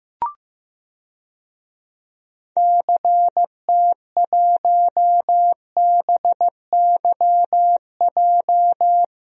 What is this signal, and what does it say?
Morse code, heard as a keyed tone, about 15 wpm, 700 Hz. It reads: CT1BYJ
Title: DXCC-100 Call Signs - 6 Characters - 15wpm